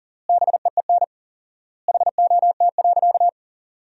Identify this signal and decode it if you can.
Morse code, heard as a keyed tone, about 37 wpm, 700 Hz. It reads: BEEN HOT.
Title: 4x Word-Spacing - Sentences from Top 500 Words - 40wpm